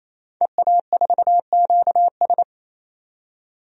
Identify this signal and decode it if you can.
Morse code, heard as a keyed tone, about 28 wpm, 700 Hz. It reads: EA4QS
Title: DXCC-100 Call Signs - 5 Characters - 28wpm